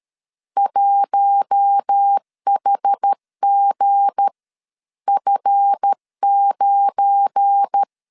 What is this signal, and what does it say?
Morse code, heard as a keyed tone, about 13 wpm, spaced as for 11 wpm, 800 Hz. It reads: 1HGF9
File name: CW / DET / 1 9 1500C S A 12wpm 3800